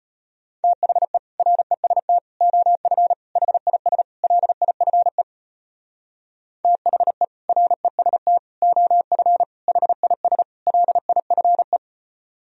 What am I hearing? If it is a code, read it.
Morse code, heard as a keyed tone, about 33 wpm, 700 Hz. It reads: THE REST OF HIS LIFE THE REST OF HIS LIFE